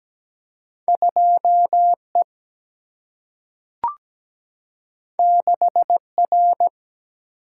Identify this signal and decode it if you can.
Morse code, heard as a keyed tone, about 17 wpm, 700 Hz. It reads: 2 E 6 R
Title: Single-Letter Single-Number 17wpm